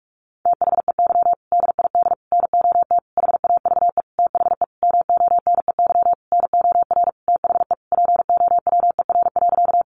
Morse code, heard as a keyed tone, about 44 wpm, 700 Hz. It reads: THEY DID NOT HAVE THE MONEY NOR THE POWER.
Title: Sentences from Top 1500 Words 45wpm